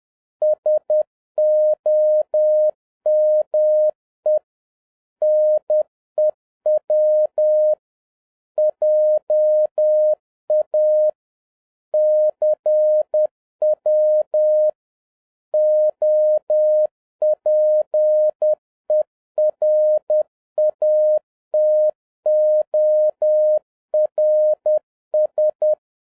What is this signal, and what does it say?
Morse code, heard as a keyed tone, about 10 wpm, 610 Hz. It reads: SOME NEW JA CW OPERATORS
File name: 2011-07-mp3 / n21 GREETINGS FROM SINGAPORE 10wpm